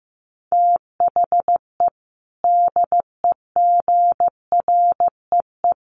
Morse code, heard as a keyed tone, about 15 wpm, 700 Hz. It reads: THE DEGREE
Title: Sets of 5 Words Encore2 15wpm